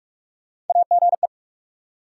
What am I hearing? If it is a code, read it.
Morse code, heard as a keyed tone, about 42 wpm, 700 Hz. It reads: AGE
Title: Sets of 3 Words - New Word Drill - 45wpm